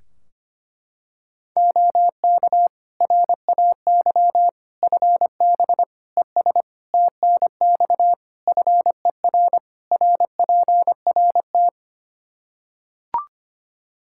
Morse code, heard as a keyed tone, about 25 wpm, 700 Hz. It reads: OK RAY FB ES TNX FER RPRT
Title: Practice QSO - Exchange 2 - Info - 25wpm